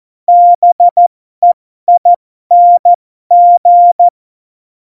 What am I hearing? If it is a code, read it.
Morse code, heard as a keyed tone, about 14 wpm, spaced as for 12 wpm, 700 Hz. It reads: BEING
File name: Morsekurs / Witze